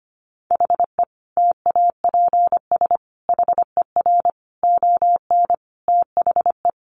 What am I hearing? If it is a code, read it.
Morse code, heard as a keyed tone, about 25 wpm, 700 Hz. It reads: HE TAPS HER ON THE
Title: Additional Easy Phrases 25wpm